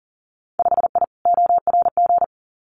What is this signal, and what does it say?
Morse code, heard as a keyed tone, about 42 wpm, 700 Hz. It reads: 5I ORG
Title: Winter Field Day Exchanges Encore 40wpm